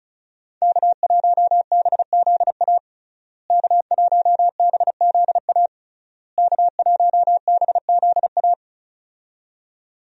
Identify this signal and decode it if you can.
Morse code, heard as a keyed tone, about 36 wpm, 700 Hz. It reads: K1BZA K1BZA K1BZA
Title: USA Call Signs - 5 Characters - Repeated 3x - 35wpm